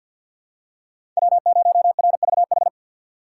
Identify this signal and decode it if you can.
Morse code, heard as a keyed tone, about 49 wpm, 700 Hz. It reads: W0RVH
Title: USA Call Signs - 5 Characters - 6x Character Spacing - 50wpm